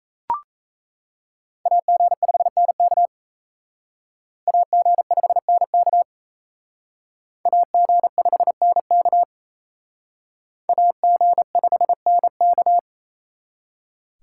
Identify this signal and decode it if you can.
Morse code, heard as a keyed tone, about 35 wpm, 700 Hz. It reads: AG5NK AG5NK AG5NK AG5NK